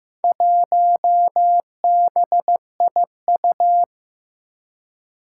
Morse code, heard as a keyed tone, about 15 wpm, 700 Hz. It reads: 1BIU
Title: Four Challenging Letter-Number Combinations 15wpm